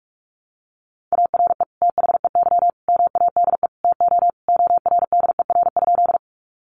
Morse code, heard as a keyed tone, about 46 wpm, 700 Hz. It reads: ARE THEY MADE TO ORDER?
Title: Sentences from Top 400 Words 45wpm